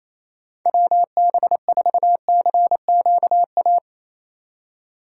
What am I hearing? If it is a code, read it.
Morse code, heard as a keyed tone, about 34 wpm, 700 Hz. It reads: WB4CQA
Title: Speed Racing - USA 2x3 Call Signs 28wpm